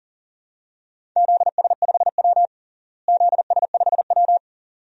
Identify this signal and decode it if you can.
Morse code, heard as a keyed tone, about 40 wpm, 700 Hz. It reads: ZS5W ZS5W
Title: DXCC-100 Call Signs - 4 Characters - Repeated 2x - 40wpm